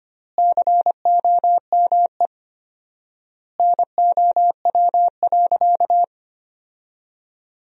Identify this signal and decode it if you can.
Morse code, heard as a keyed tone, about 25 wpm, 700 Hz. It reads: COME NOW.
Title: 4x Word-Spacing - Sentences from Top 200 Words - 25wpm